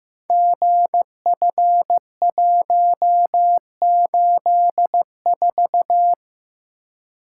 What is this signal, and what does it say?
Morse code, heard as a keyed tone, about 15 wpm, 700 Hz. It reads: GF184
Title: Code Groups - 5 Letter:Numbers - 15wpm